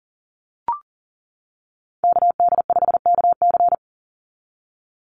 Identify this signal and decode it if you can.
Morse code, heard as a keyed tone, about 40 wpm, 700 Hz. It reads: KD5KC
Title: NA SOTA Activator Callsigns 40wpm